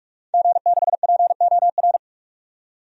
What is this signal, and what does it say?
Morse code, heard as a keyed tone, about 43 wpm, 700 Hz. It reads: G6POR
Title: DXCC-100 Call Signs - 5 Characters - 45wpm